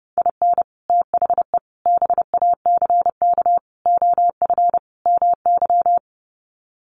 Morse code, heard as a keyed tone, about 30 wpm, 700 Hz. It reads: IN THE BACK OF MY
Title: Sets of 5 Words Encore 30wpm